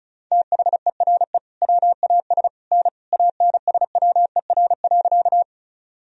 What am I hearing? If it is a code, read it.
Morse code, heard as a keyed tone, about 36 wpm, 700 Hz. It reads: THERE WAS N ANSWER.
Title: Sentences from Top 300 Words and Abbreviations 35wpm